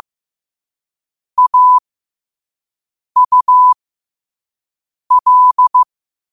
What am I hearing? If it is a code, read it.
Morse code, heard as a keyed tone, about 15 wpm, spaced as for 5 wpm, 1000 Hz. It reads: AUL
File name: qotdfiles / 5wpm